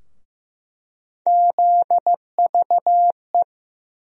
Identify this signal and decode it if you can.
Morse code, heard as a keyed tone, about 15 wpm, 700 Hz. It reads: ZVE
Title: USA Call Sign Postfixes - 3 Letters 15wpm